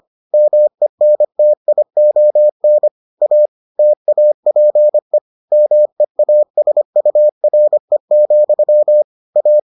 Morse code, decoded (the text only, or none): MENTION A TAPE MEASURE, A